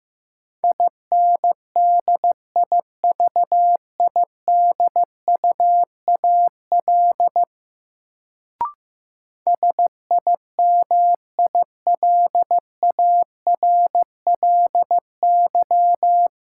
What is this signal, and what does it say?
Morse code, heard as a keyed tone, about 15 wpm, 700 Hz. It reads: INDIVIDUAL SIMILARLY
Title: Sentences from Top 1400 Words - New Word Drill Alt - 15wpm